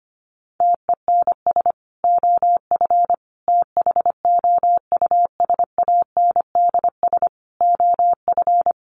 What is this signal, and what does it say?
Morse code, heard as a keyed tone, about 25 wpm, 700 Hz. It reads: TENS OF THOUSANDS OF
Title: Sets of 4 Words 25wpm